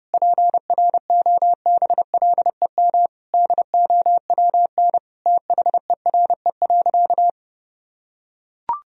Morse code, heard as a keyed tone, about 30 wpm, 700 Hz. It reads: PROBLEM DOWN THERE.
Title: Sentences from Top 1000 Words - Review Part 4 - 30wpm